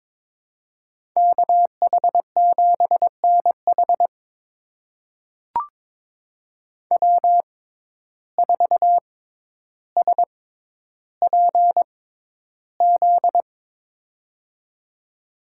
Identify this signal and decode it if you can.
Morse code, heard as a keyed tone, about 22 wpm, spaced as for 11 wpm, 700 Hz. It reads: KH7NH W4SPZ
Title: USA Call Signs - 6x Character Spacing - 22wpm